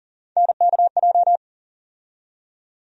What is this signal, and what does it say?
Morse code, heard as a keyed tone, about 40 wpm, 700 Hz. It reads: NKJ
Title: USA Call Sign Postfixes - 3 Letters 40wpm